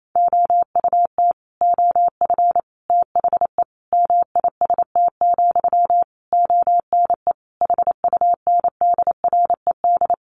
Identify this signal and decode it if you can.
Morse code, heard as a keyed tone, about 28 wpm, 700 Hz. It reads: OUT OF THE MIST, ONE HUNDRED
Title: A Sound Of Thunder 28wpm